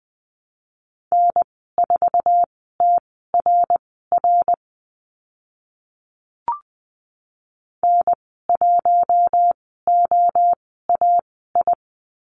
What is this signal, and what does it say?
Morse code, heard as a keyed tone, about 20 wpm, spaced as for 14 wpm, 700 Hz. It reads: N4TRR N1OAI
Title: 076 Lesson 13g - Letter L - Call Signs 20wpm